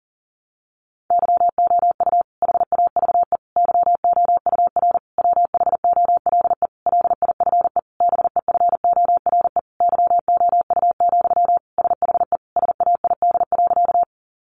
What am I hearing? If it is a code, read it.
Morse code, heard as a keyed tone, about 40 wpm, 700 Hz. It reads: YOU HAVE YOUR WHOLE LIFE BEFORE YOU, SHE SAID.
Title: Sentences from Top 400 Words 40wpm